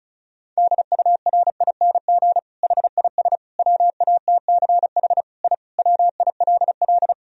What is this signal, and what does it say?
Morse code, heard as a keyed tone, about 36 wpm, 700 Hz. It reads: DURING HIS WATCH I WILL
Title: Sentences from Top 1000 Words - Review Part 2 - 35wpm